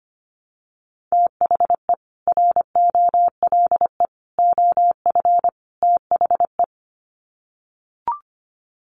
Morse code, heard as a keyed tone, about 25 wpm, 700 Hz. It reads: THE ROLE OF THE
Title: Sets of 4 Words 25wpm